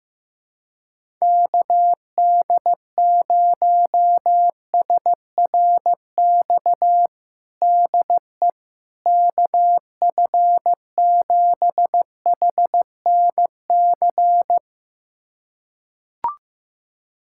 Morse code, heard as a keyed tone, about 15 wpm, 700 Hz. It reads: KD0SRX DE KF7HNC
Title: USA Callsign DE Callsign - 6 Characters - 15wpm